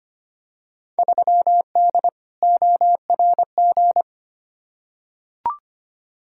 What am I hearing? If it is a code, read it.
Morse code, heard as a keyed tone, about 25 wpm, 700 Hz. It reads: 3D ORG
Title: Field Day Exchanges Encore - 25wpm